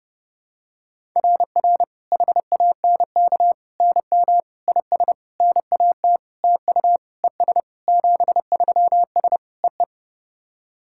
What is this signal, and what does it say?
Morse code, heard as a keyed tone, about 30 wpm, 700 Hz. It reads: RR HANK NM IS NAT TU ES 73S EE
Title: SOTA QSO Practice 30wpm